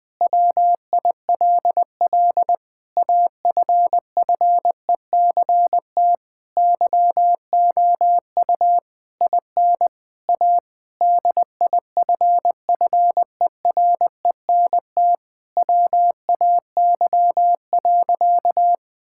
Morse code, heard as a keyed tone, about 21 wpm, 700 Hz. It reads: WILL AFFECT YOU IN A DIFFERENT WAY.
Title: Sentences from Top 1000 Words - Review Part 6 - 20wpm